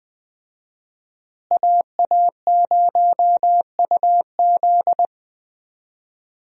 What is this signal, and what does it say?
Morse code, heard as a keyed tone, about 20 wpm, 700 Hz. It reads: AA0UZ
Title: USA Call Signs - 5 Characters - Repeated 2x - 20wpm